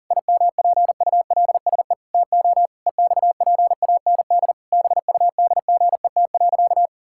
Morse code, heard as a keyed tone, about 40 wpm, 700 Hz. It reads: IMPULSE TO EXPAND BUDGET.
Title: Pangram Challenge v2 - 40wpm